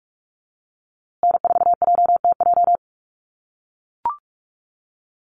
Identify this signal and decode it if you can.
Morse code, heard as a keyed tone, about 44 wpm, 690 Hz. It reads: N4JTJ
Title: USA Call Signs Repeated 3x 45wpm